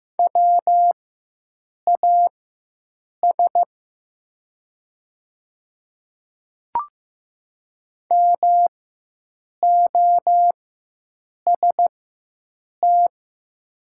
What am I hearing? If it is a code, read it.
Morse code, heard as a keyed tone, about 15 wpm, spaced as for 7 wpm, 700 Hz. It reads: WAS MOST
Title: 4x Character Spacing - Top 100 Words - 15wpm